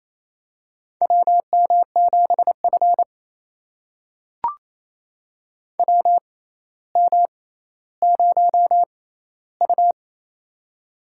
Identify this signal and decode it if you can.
Morse code, heard as a keyed tone, about 28 wpm, spaced as for 14 wpm, 700 Hz. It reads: WM7F WM0U
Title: USA Call Signs - 4 Characters - 6x Character Spacing - 28wpm